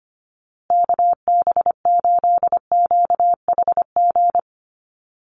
Morse code, heard as a keyed tone, about 25 wpm, 700 Hz. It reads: KB8QHG
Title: USA Call Signs Repeated 2x 25wpm